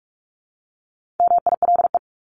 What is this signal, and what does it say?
Morse code, heard as a keyed tone, about 44 wpm, 700 Hz. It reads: MILE
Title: ES:S Suffix - Top 100 Words with Base Words 45wpm